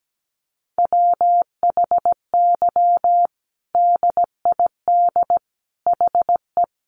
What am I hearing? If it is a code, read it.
Morse code, heard as a keyed tone, about 17 wpm, 700 Hz. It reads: WHY DID HE
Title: Sentences from Top 200 Words 17wpm